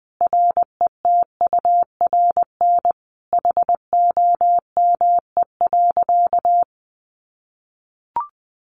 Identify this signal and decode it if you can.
Morse code, heard as a keyed tone, about 20 wpm, 700 Hz. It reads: RETURN HOME.
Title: Sentences from Top 800 Words 20wpm